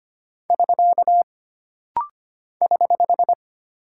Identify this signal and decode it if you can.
Morse code, heard as a keyed tone, about 25 wpm, 700 Hz. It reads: <SK> <HH>
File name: Single Punctuation-Prosign - Rapid-Fire - 25wpm